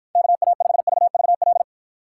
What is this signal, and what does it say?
Morse code, heard as a keyed tone, about 52 wpm, 700 Hz. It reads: KA5VVL